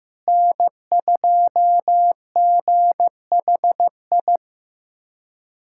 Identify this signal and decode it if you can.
Morse code, heard as a keyed tone, about 15 wpm, 700 Hz. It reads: N2GHI
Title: USA Callsign DE Callsign 15wpm